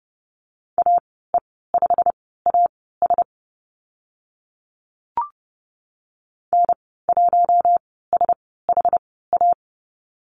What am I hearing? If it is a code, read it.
Morse code, heard as a keyed tone, about 30 wpm, spaced as for 17 wpm, 700 Hz. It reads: AE5AS N1SHA
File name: Lesson 15g - Number 5 - Call Signs 30wpm